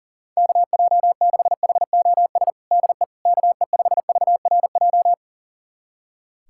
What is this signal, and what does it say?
Morse code, heard as a keyed tone, about 40 wpm, 700 Hz. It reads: KJ6HOS DE KE5VRJ
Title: USA Callsign DE Callsign - 6 Characters - 40wpm